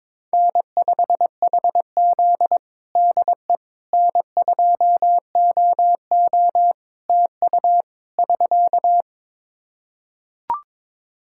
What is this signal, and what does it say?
Morse code, heard as a keyed tone, about 22 wpm, 700 Hz. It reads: N5HZ DE N2OO TU <SK>
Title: Practice QSO - Exchange 4 - Ending - 22wpm